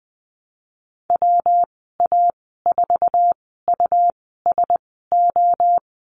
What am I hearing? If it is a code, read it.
Morse code, heard as a keyed tone, about 20 wpm, spaced as for 15 wpm, 700 Hz. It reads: WA4USO
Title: 154 Lesson 25g - Letter p - Call Signs 20wpm